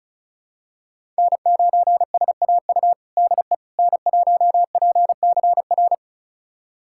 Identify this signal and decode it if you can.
Morse code, heard as a keyed tone, about 36 wpm, 700 Hz. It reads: N9SAU DE N1PCR
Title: USA Callsign DE Callsign - 5 Characters - 35wpm